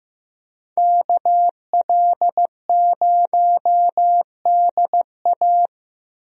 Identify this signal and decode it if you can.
Morse code, heard as a keyed tone, about 15 wpm, 700 Hz. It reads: KL0DA
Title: USA Call Signs - 5 Characters - Repeated 2x - 15wpm